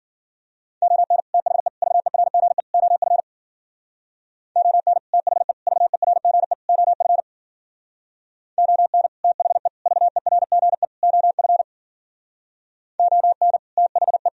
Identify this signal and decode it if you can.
Morse code, heard as a keyed tone, about 52 wpm, 700 Hz. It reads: ON THE VERGE OF ON THE VERGE OF ON THE VERGE OF ON THE